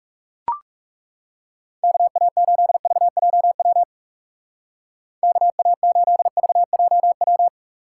Morse code, heard as a keyed tone, about 37 wpm, 700 Hz. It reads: KA8VJW KA8VJW
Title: Speed Racing - USA Call Signs 30wpm